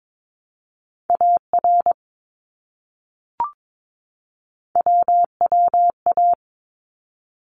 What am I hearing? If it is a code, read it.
Morse code, heard as a keyed tone, about 22 wpm, 700 Hz. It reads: AR WWA
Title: ARRL Sections 22wpm